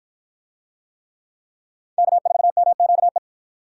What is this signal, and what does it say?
Morse code, heard as a keyed tone, about 51 wpm, 700 Hz. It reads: K4MQE